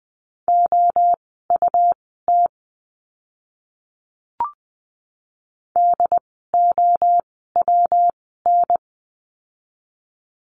Morse code, heard as a keyed tone, about 20 wpm, spaced as for 15 wpm, 700 Hz. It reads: OUT DOWN
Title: 244 Lesson 39f - Letter z - Words 20wpm